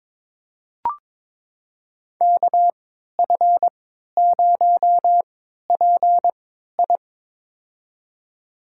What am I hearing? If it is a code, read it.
Morse code, heard as a keyed tone, about 22 wpm, spaced as for 13 wpm, 700 Hz. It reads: KF0PI
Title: USA Call Signs - 5 Characters - 3x Character Spacing - 22wpm